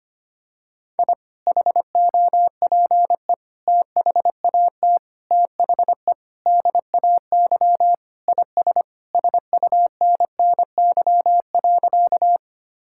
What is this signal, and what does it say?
Morse code, heard as a keyed tone, about 25 wpm, 700 Hz. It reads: I HOPE THAT THE DAY IS SUNNY.
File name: practice / 25wpm / Sentences with 2-Word Phrases - 25wpm